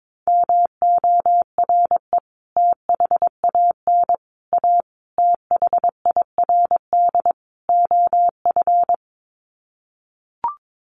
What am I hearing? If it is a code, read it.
Morse code, heard as a keyed tone, about 22 wpm, 700 Hz. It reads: MORE THAN A THIRD OF
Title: Sets of 5 Words Encore2 22wpm